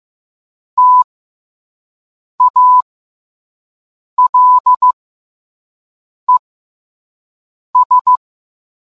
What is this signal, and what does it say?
Morse code, heard as a keyed tone, about 15 wpm, spaced as for 5 wpm, 1000 Hz. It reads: TALES